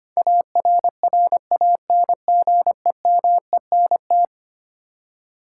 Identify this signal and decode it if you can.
Morse code, heard as a keyed tone, about 25 wpm, 700 Hz. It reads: ARRANGEMENT
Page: morse-code-ninja-practice-25wpm